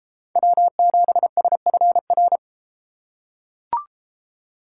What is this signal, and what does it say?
Morse code, heard as a keyed tone, about 33 wpm, 700 Hz. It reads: W7SFR